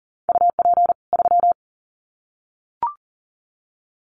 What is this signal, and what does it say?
Morse code, heard as a keyed tone, about 40 wpm, 700 Hz. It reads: UP 3
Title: Common QSO Phrases 40wpm